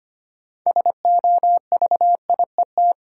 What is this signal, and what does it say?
Morse code, heard as a keyed tone, about 25 wpm, 700 Hz. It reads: SOVIET
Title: Sets of 5 Words Encore - New Word Drill - 25wpm